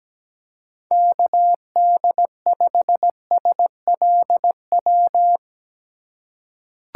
Morse code, heard as a keyed tone, about 17 wpm, 700 Hz. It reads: KD5SLW
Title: USA Call Signs - 3x Character Spacing - 17wpm